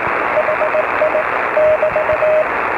Morse code, heard as a keyed tone, about 18 wpm, 620 Hz, with radio noise. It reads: HI <BT>